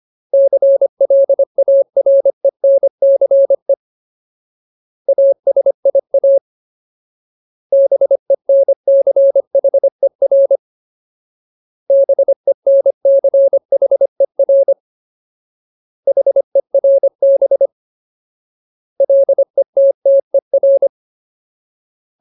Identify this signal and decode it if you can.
Morse code, heard as a keyed tone, about 25 wpm, 550 Hz. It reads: CLARENCE ASIA BENCHER BENCHER HERB LETTER
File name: Ultimate Word List / 25WPM Ultimate